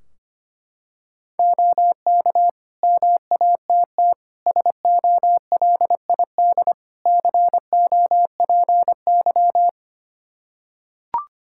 Morse code, heard as a keyed tone, about 25 wpm, 700 Hz. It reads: OK MATT SOLID COPY
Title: Practice QSO - Exchange 3 - More Info - 25wpm